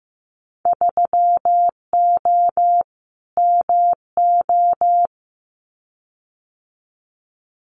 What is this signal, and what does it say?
Morse code, heard as a keyed tone, about 15 wpm, 700 Hz. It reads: 3O MO
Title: Winter Field Day Exchanges Encore 15wpm